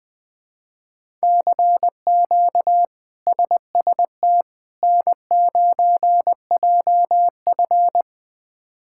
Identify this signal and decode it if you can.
Morse code, heard as a keyed tone, about 20 wpm, 700 Hz. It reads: CQ SST N9JF
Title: K1USN SST QSO Practice - 20wpm